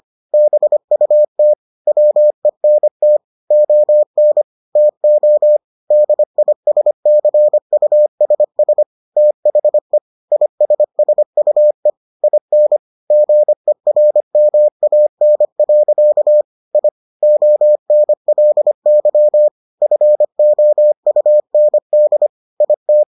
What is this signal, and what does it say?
Morse code, heard as a keyed tone, about 25 wpm, 600 Hz. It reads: BUT WENT ON TO DISCUSS THE ISSUE IN GERMAN. I ONLY FOUND IT